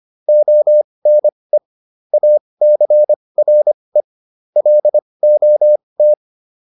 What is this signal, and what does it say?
Morse code, heard as a keyed tone, about 25 wpm, spaced as for 20 wpm, 600 Hz. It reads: ONE ACRE LOT
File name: PR304 20